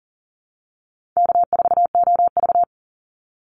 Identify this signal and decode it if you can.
Morse code, heard as a keyed tone, about 40 wpm, 700 Hz. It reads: K4OV
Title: Easy Call Signs - 40wpm